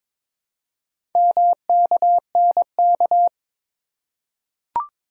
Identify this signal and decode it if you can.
Morse code, heard as a keyed tone, about 22 wpm, 700 Hz. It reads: MKNK
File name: practice / 22wpm / Four Letter-Number Combinations - 22wpm